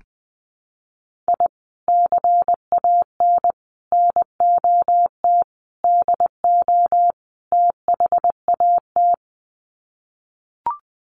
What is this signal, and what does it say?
Morse code, heard as a keyed tone, about 20 wpm, 700 Hz. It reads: I CAN NOT DO THAT
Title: Sets of 5 Words Encore2 20wpm